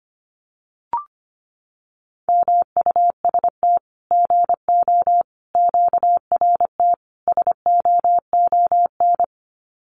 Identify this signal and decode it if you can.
Morse code, heard as a keyed tone, about 25 wpm, 700 Hz. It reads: MUST GO QRT SOON
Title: Common QSO Phrases 25wpm